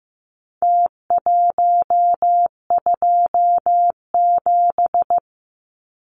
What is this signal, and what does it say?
Morse code, heard as a keyed tone, about 15 wpm, 700 Hz. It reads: T127
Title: Numbers - Contest Style - 4-Digit 15wpm